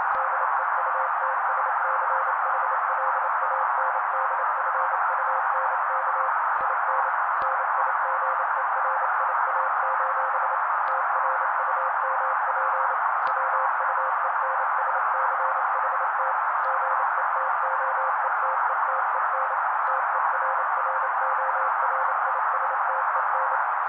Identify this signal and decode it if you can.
Morse code, heard as a keyed tone, about 29 wpm, 530 Hz. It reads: DEUTSCHLANDFUNK IN NIGERIA: TRUMP WUENSCHT GETOETETEN TERRORISTEN
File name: daily / 27wpm noise